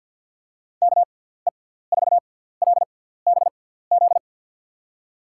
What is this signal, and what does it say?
Morse code, heard as a keyed tone, about 52 wpm, spaced as for 17 wpm, 700 Hz. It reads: KE4LBZ
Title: USA Call Signs - 6 Characters - 6x Character Spacing - 50wpm